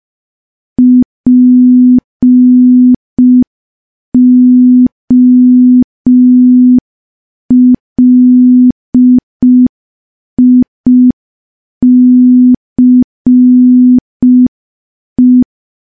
Morse code, heard as a keyed tone, about 5 wpm, 250 Hz. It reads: POLICE
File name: morse / morse-training-05WPM